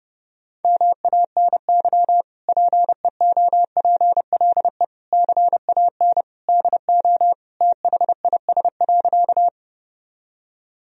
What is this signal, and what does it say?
Morse code, heard as a keyed tone, about 30 wpm, 700 Hz. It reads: MANY PEOPLE CAN DO THIS.
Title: Sentences from Top 1000 Words - Review Part 1 - 30wpm